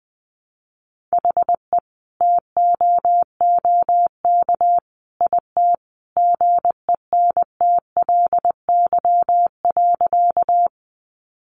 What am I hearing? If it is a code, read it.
Morse code, heard as a keyed tone, about 20 wpm, 700 Hz. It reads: HE TOOK IT GENTLY.
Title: Sentences from Top 1500 Words 20wpm